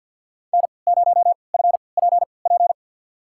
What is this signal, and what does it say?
Morse code, heard as a keyed tone, about 49 wpm, spaced as for 28 wpm, 700 Hz. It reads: N0FPP
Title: USA Call Signs - 5 Characters - 3x Character Spacing - 50wpm